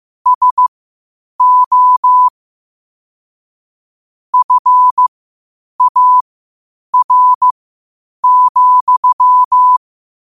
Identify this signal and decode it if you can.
Morse code, heard as a keyed tone, about 15 wpm, spaced as for 8 wpm, 1000 Hz. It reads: SO FAR,
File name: qotdfiles / 8wpm